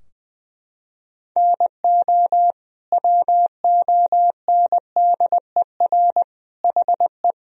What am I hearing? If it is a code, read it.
Morse code, heard as a keyed tone, about 20 wpm, 700 Hz. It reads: NO WONDER HE
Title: Sentences from Top 1000 Words - Review Part 3 - 20wpm